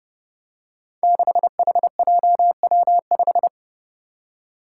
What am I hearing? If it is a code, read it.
Morse code, heard as a keyed tone, about 30 wpm, 700 Hz. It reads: 6HJW5
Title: Five Challenging Letter-Number Combinations 30wpm